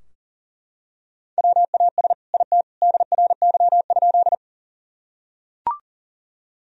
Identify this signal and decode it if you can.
Morse code, heard as a keyed tone, about 40 wpm, 700 Hz. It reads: WAS IT DRY?
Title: Sentences from Top 1000 Words - Review Part 3 - 40wpm